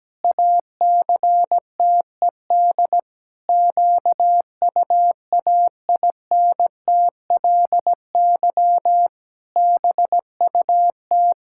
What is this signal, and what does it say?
Morse code, heard as a keyed tone, about 17 wpm, 700 Hz. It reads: ACTED QUAINTLY BUT
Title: Pangram Challenge v2 - 17wpm